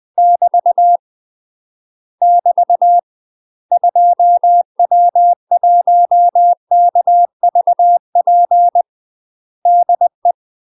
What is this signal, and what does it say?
Morse code, heard as a keyed tone, about 20 wpm, 700 Hz. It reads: <BT> <BT> 2W1KVP DE
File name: UK MASTER QSOs TRAINING / 20WPM / UK MASTER QSOs 0016